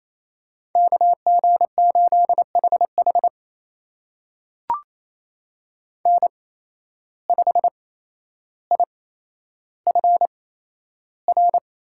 Morse code, heard as a keyed tone, about 28 wpm, spaced as for 12 wpm, 700 Hz. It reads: KG8HH N5IFR
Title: USA Call Signs - 5 Characters - 8x Character Spacing - 28wpm